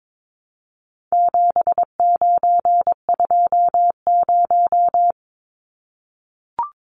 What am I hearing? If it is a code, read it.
Morse code, heard as a keyed tone, about 22 wpm, 700 Hz. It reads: 7920
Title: Numbers - 4-Digit 22wpm